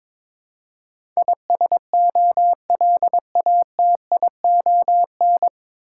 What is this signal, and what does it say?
Morse code, heard as a keyed tone, about 22 wpm, 690 Hz. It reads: ISOLATION
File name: 9 Letter Words - First 3 Letters Given - 22wpm